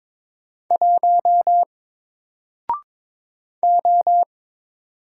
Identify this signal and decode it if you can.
Morse code, heard as a keyed tone, about 22 wpm, 700 Hz. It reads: 1 O
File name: Single-Letter-Number-22wpm